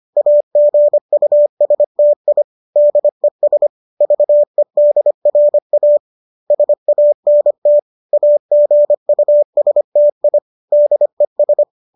Morse code, heard as a keyed tone, about 25 wpm, 590 Hz. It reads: AGUSTI DES VEDRA SANT AGUSTI DES